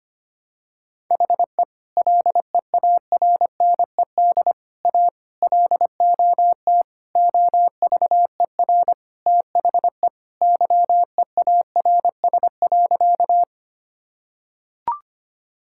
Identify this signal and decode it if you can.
Morse code, heard as a keyed tone, about 25 wpm, 700 Hz. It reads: HE LEARNED A LOT OVER THE YEARS.
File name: practice / 25wpm / Sentences with 3-Word Phrases - 25wpm